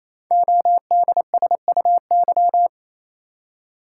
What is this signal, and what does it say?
Morse code, heard as a keyed tone, about 29 wpm, 700 Hz. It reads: ODSUY
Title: Code Groups - 5 Letters - 28wpm